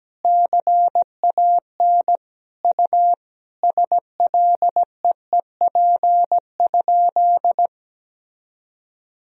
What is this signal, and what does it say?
Morse code, heard as a keyed tone, about 17 wpm, 700 Hz. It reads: CAN U SLEEP?
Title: Sentences from Top 500 Words and Abbreviations 17wpm